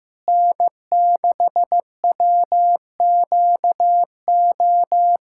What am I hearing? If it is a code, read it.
Morse code, heard as a keyed tone, about 15 wpm, 700 Hz. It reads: N6WQO
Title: USA Callsign DE Callsign 15wpm